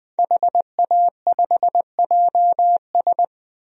Morse code, heard as a keyed tone, about 20 wpm, 700 Hz. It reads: HA5JS
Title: DXCC-100 Call Signs - 5 Characters - 20wpm